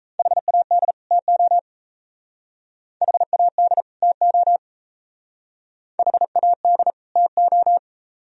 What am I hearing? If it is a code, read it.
Morse code, heard as a keyed tone, about 37 wpm, 700 Hz. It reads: HAD TO HAD TO HAD TO